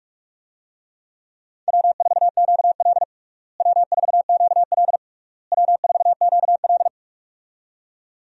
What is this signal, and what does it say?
Morse code, heard as a keyed tone, about 42 wpm, 700 Hz. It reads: W4QL W4QL W4QL
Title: USA Call Signs - 4 Characters - Repeated 3x - 45wpm